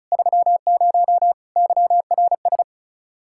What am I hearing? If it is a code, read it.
Morse code, heard as a keyed tone, about 36 wpm, 700 Hz. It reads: 30 YRS
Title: Practice QSO - Exchange 3 - More Info - 35wpm